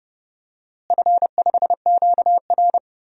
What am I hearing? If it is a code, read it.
Morse code, heard as a keyed tone, about 30 wpm, 700 Hz. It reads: F5QR
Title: DXCC-100 Call Signs - 4 Characters - Repeated 2x - 30wpm